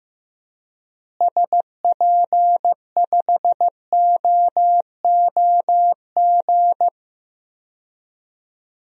Morse code, heard as a keyed tone, about 15 wpm, 700 Hz. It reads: SP5OOG
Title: DXCC-100 Call Signs - 6 Characters - Repeated 3x - 15wpm